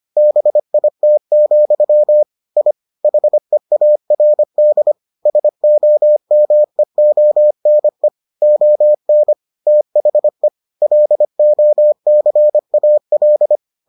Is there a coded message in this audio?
Yes: Morse code from 0.2 to 13.6 s, heard as a keyed tone, about 25 wpm, 600 Hz. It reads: BIT, I HEARD SOMEONE ON THE LOCAL